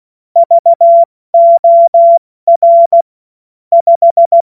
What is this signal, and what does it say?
Morse code, heard as a keyed tone, about 16 wpm, spaced as for 14 wpm, 680 Hz. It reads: VOR 5
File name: Morsekurs / Witze / Witz 19. April 2022 70 F80